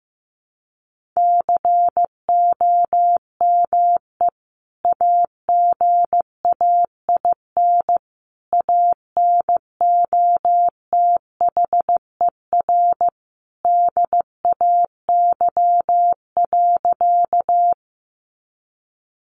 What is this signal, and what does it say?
Morse code, heard as a keyed tone, about 15 wpm, 700 Hz. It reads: COME AGAIN ANOTHER DAY.
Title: Sentences from Top 600 Words 15wpm